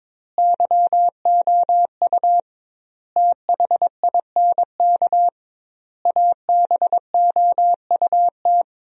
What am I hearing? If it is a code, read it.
Morse code, heard as a keyed tone, about 22 wpm, 700 Hz. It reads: YOU THINK ABOUT
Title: 2x Word-Spacing - Sentences from Top 200 Words - 22wpm